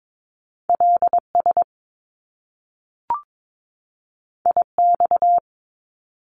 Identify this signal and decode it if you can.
Morse code, heard as a keyed tone, about 22 wpm, 700 Hz. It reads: LS IX
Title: Top Two Letter Combinations in Words 22wpm